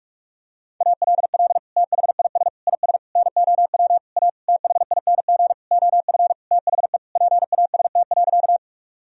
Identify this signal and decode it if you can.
Morse code, heard as a keyed tone, about 41 wpm, 700 Hz. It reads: ALL THIS IS NOW A THING OF THE PAST.
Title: Sentences from Top 1000 Words - Review Part 3 - 45wpm